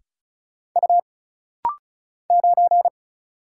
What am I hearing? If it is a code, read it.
Morse code, heard as a keyed tone, about 36 wpm, 700 Hz. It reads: U 9
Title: Single Letter-Number - Rapid-Fire 35wpm